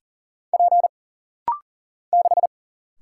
Morse code, heard as a keyed tone, about 40 wpm, 700 Hz. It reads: P 6
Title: Single Letter-Number - Rapid-Fire 40wpm